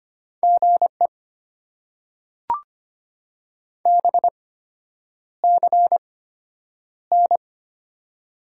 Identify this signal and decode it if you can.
Morse code, heard as a keyed tone, about 25 wpm, spaced as for 9 wpm, 700 Hz. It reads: GE BCN
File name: Top 90 QSO Elements - 8x Character Spacing - 25wpm